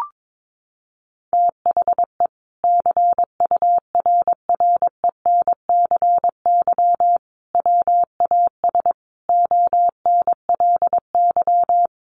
Morse code, heard as a keyed tone, about 22 wpm, 700 Hz. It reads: THE CURRENCY WAS ONLY
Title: Sentences from Top 1500 Words 22wpm